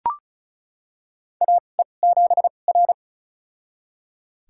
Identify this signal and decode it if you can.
Morse code, heard as a keyed tone, about 36 wpm, spaced as for 25 wpm, 690 Hz. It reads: AE7R
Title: USA Call Signs - 4 Characters - 2x Character Spacing - 35wpm